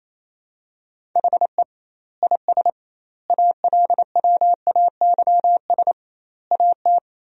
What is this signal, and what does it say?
Morse code, heard as a keyed tone, about 28 wpm, 700 Hz. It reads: HE IS ALWAYS AT
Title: 2x Word-Spacing - Sentences from Top 300 Words - 28wpm